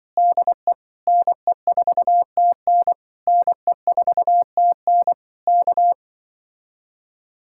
Morse code, heard as a keyed tone, about 23 wpm, 700 Hz. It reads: DE NE4TN NE4TN K
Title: Speed-Racing Common QSO Phrases - 20wpm